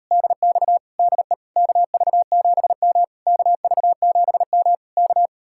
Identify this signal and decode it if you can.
Morse code, heard as a keyed tone, about 38 wpm, 700 Hz. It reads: DX DE KV7M KV7M K